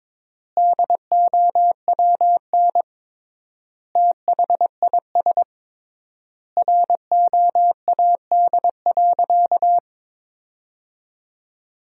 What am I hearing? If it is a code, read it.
Morse code, heard as a keyed tone, about 22 wpm, 700 Hz. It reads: DOWN THIS ROAD.
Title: 3x Word-Spacing - Sentences from Top 400 Words - 22wpm